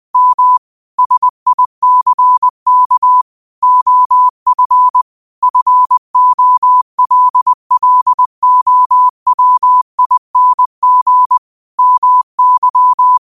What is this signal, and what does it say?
Morse code, heard as a keyed tone, about 20 wpm, 1000 Hz. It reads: M SICK OF FOLLOWING MY